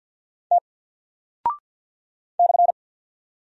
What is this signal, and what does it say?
Morse code, heard as a keyed tone, about 48 wpm, 700 Hz. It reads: T /
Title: Single Character (Letter:Number:Punctuation:Prosign) Alt 50wpm